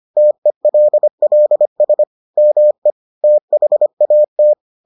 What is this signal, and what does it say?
Morse code, heard as a keyed tone, about 25 wpm, 600 Hz. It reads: TELLS ME THAT